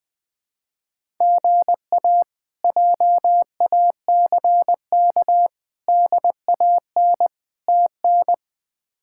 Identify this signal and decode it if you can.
Morse code, heard as a keyed tone, about 20 wpm, 700 Hz. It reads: GA JACK DAN TN
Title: K1USN SST QSO Practice - 20wpm